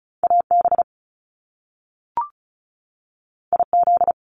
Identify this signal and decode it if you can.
Morse code, heard as a keyed tone, about 36 wpm, 700 Hz. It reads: AB IZ
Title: Top Two Letter Combinations in Words 35wpm